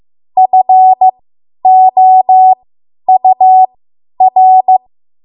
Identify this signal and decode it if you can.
Morse code, heard as a keyed tone, about 15 wpm, spaced as for 10 wpm, 750 Hz. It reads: FOUR
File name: Morse / 10WPM